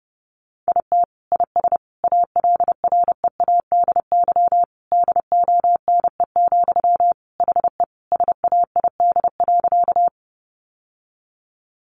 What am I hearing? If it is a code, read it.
Morse code, heard as a keyed tone, about 30 wpm, 700 Hz. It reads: IT IS ALREADY DONE, HE SAID.
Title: Sentences from Top 700 Words 30wpm